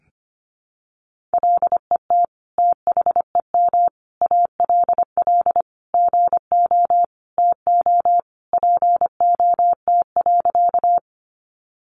Morse code, heard as a keyed tone, about 25 wpm, 700 Hz. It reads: LET THEM ALL GO TO POT.
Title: Sentences from Top 1000 Words 25wpm